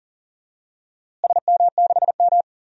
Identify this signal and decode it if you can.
Morse code, heard as a keyed tone, about 39 wpm, 700 Hz. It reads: SM6M
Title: DXCC-100 Call Signs - 4 Characters - Repeated 2x - 40wpm